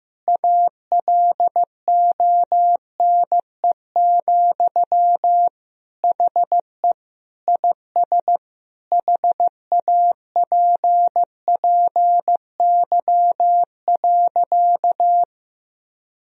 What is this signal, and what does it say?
Morse code, heard as a keyed tone, about 15 wpm, 700 Hz. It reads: ALONE, HE IS HAPPY.
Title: Sentences from Top 1000 Words - Review Part 3 - 15wpm